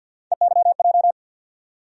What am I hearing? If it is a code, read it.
Morse code, heard as a keyed tone, about 49 wpm, 700 Hz. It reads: EYJ
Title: USA Call Sign Postfixes - 3 Letters 50wpm